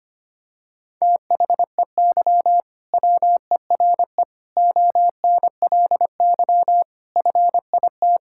Morse code, heard as a keyed tone, about 25 wpm, 700 Hz. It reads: THEY WERE ONLY FIT
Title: Sentences from Top 1000 Words - Review Part 5 - 25wpm